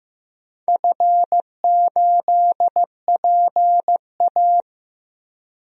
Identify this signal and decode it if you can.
Morse code, heard as a keyed tone, about 15 wpm, 700 Hz. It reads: F8PA
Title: DXCC-100 Call Signs - 4 Characters - Repeated 2x - 15wpm